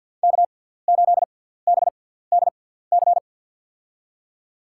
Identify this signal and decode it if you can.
Morse code, heard as a keyed tone, about 54 wpm, spaced as for 17 wpm, 700 Hz. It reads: K8BDC
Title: USA Call Signs - 5 Characters - 6x Character Spacing - 50wpm